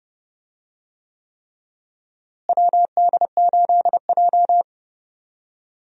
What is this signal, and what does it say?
Morse code, heard as a keyed tone, about 30 wpm, 700 Hz. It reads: WD8J
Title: USA Call Signs - 4 Characters - Repeated 3x - 30wpm